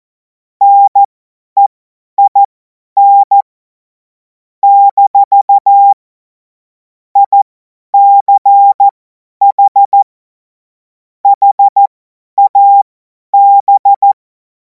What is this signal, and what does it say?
Morse code, heard as a keyed tone, about 14 wpm, spaced as for 10 wpm, 800 Hz. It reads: NEIN - ICH HAB
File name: Morsekurs / Witze / Witz 31. März 2026 50 F70